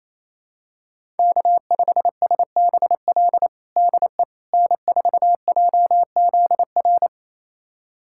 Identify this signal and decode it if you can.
Morse code, heard as a keyed tone, about 28 wpm, 700 Hz. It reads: K5SBL DE N4JZR
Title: USA Callsign DE Callsign - 5 Characters - 28wpm